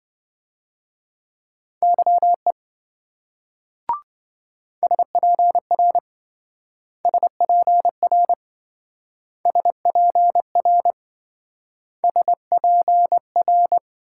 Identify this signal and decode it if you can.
Morse code, heard as a keyed tone, about 26 wpm, 700 Hz. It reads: YE SPR SPR SPR SPR